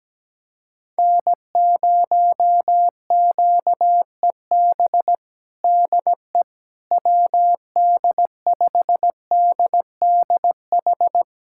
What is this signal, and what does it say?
Morse code, heard as a keyed tone, about 17 wpm, 700 Hz. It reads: N0QEB DE WD5DDH
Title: USA Callsign DE Callsign 17wpm